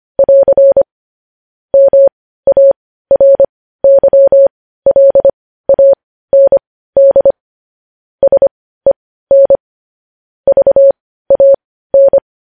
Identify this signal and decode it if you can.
Morse code, heard as a keyed tone, about 25 wpm, spaced as for 15 wpm, 550 Hz. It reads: <AR> MARYLAND SEN VAN
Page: news-headlines-in-morse-code-at-15-wpm